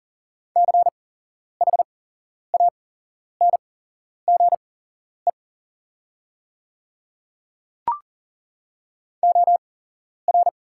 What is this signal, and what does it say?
Morse code, heard as a keyed tone, about 40 wpm, spaced as for 11 wpm, 690 Hz. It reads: CHANGE OR
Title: 8x Character Spacing - Top 300 Words - 40wpm